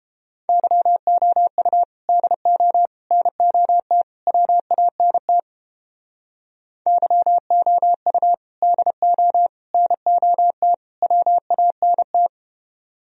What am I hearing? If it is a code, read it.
Morse code, heard as a keyed tone, about 29 wpm, 700 Hz. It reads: YOU DO NOT WANT YOU DO NOT WANT